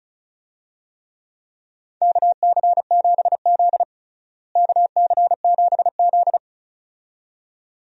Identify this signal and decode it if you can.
Morse code, heard as a keyed tone, about 34 wpm, 700 Hz. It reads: KC7Z KC7Z
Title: USA Call Signs - 4 Characters - Repeated 2x - 35wpm